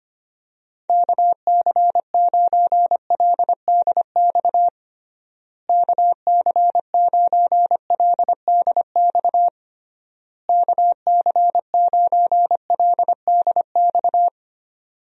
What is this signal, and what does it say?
Morse code, heard as a keyed tone, about 25 wpm, 700 Hz. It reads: KC9LDX KC9LDX KC9LDX
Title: USA Call Signs - 6 Characters - Repeated 3x - 25wpm